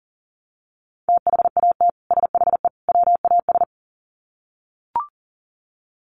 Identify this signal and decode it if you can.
Morse code, heard as a keyed tone, about 40 wpm, 700 Hz. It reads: THAT SHE WAS
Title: Sets of 3 Words 40wpm